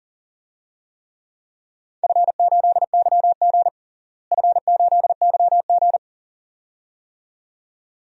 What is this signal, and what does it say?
Morse code, heard as a keyed tone, about 37 wpm, 700 Hz. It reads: F8YG F8YG
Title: DXCC-100 Call Signs - 4 Characters - Repeated 2x - 40wpm